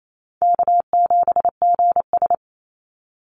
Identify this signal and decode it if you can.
Morse code, heard as a keyed tone, about 28 wpm, 700 Hz. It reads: K7GS
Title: Easy Call Signs Repeated 3x - 28wpm